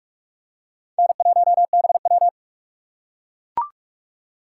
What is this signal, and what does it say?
Morse code, heard as a keyed tone, about 41 wpm, 700 Hz. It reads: N1BW
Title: USA Call Signs - 2x Character Spacing - 45wpm